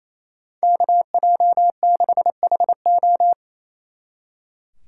Morse code, heard as a keyed tone, about 35 wpm, 700 Hz. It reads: KJ6HO